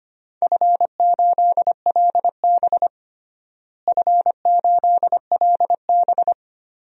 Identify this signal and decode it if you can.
Morse code, heard as a keyed tone, about 25 wpm, 700 Hz. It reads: F8LB F8LB
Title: DXCC-100 Call Signs - 4 Characters - Repeated 2x - 25wpm